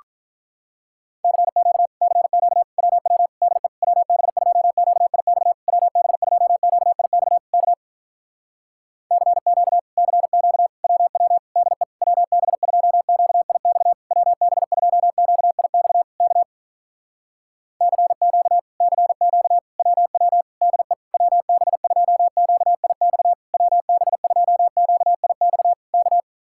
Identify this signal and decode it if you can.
Morse code, heard as a keyed tone, about 48 wpm, 700 Hz. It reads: CQ CQ WW DE WB2QIX WB2QIX K CQ CQ WW DE WB2QIX WB2QIX K CQ CQ WW DE WB2QIX WB2QIX K